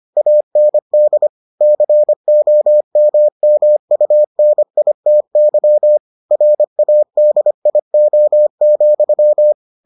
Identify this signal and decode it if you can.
Morse code, heard as a keyed tone, about 25 wpm, 600 Hz. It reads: AND COMMUNITY RADIO,